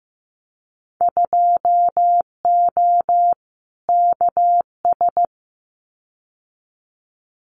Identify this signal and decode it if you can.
Morse code, heard as a keyed tone, about 15 wpm, 700 Hz. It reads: 2O KS
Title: Winter Field Day Exchanges Encore 15wpm